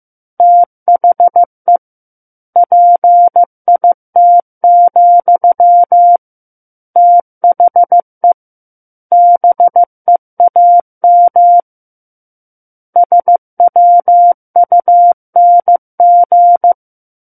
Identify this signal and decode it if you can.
Morse code, heard as a keyed tone, about 15 wpm, 700 Hz. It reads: THE PIT, THE BEAM SWUNG
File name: Chapter 6 - 15 WPM